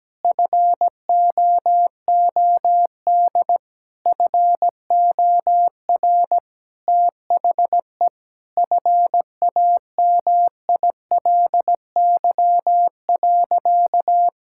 Morse code, heard as a keyed tone, about 17 wpm, 700 Hz. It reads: FOOD FOR THE FAMILY.
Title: Sentences from Top 1600 Words 17wpm